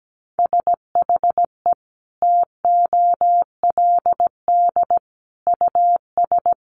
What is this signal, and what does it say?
Morse code, heard as a keyed tone, about 17 wpm, 700 Hz. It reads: SHE TOLD US
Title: Sentences from Top 1100 Words 17wpm